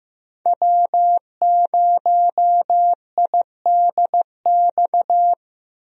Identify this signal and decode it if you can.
Morse code, heard as a keyed tone, about 15 wpm, 700 Hz. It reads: W0IDX
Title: USA Call Signs - 5 Characters - Repeated 3x - 15wpm